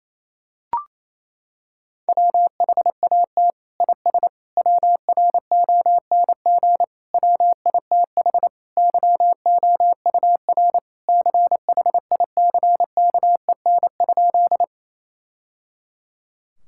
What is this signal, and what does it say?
Morse code, heard as a keyed tone, about 27 wpm, 700 Hz. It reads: WHAT IS WRONG WITH YOUR CHICKEN?
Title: Sentences from Top 1000 Words - Review Part 5 - 28wpm